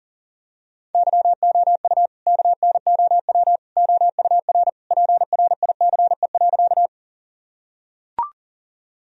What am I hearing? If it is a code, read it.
Morse code, heard as a keyed tone, about 40 wpm, 700 Hz. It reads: YOU KNOW OUR PRICE.
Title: Sentences from Top 1000 Words - Review Part 4 - 40wpm